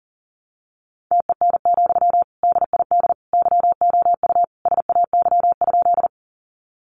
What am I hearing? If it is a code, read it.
Morse code, heard as a keyed tone, about 40 wpm, 700 Hz. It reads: TEN, DID YOU SAY?
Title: Sentences from Top 400 Words 40wpm